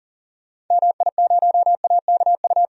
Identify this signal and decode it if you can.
Morse code, heard as a keyed tone, about 40 wpm, 700 Hz. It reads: MI0AKU
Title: DXCC-100 Call Signs - 6 Characters - 40wpm